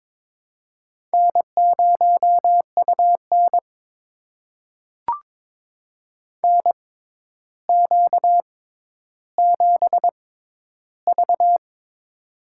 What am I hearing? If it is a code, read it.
Morse code, heard as a keyed tone, about 22 wpm, spaced as for 11 wpm, 700 Hz. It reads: N0UN NQ7V
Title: USA Call Signs - 4 Characters - 6x Character Spacing - 22wpm